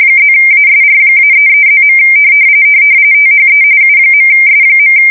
RTTY, about 45.45 baud. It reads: +ASR-33+. THE U.S.
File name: fsk170 / teletype-corp